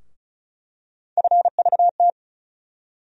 Morse code, heard as a keyed tone, about 36 wpm, 700 Hz. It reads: FVT
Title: USA Call Sign Postfixes - 3 Letters 35wpm